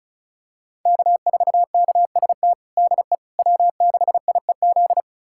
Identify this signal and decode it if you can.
Morse code, heard as a keyed tone, about 36 wpm, 700 Hz. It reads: K4KST DE W6IEZ
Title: USA Callsign DE Callsign - 5 Characters - 35wpm